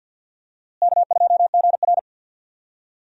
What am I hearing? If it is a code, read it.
Morse code, heard as a keyed tone, about 47 wpm, 700 Hz. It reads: K2GR
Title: USA Call Signs - 4 Characters - Repeated 2x - 50wpm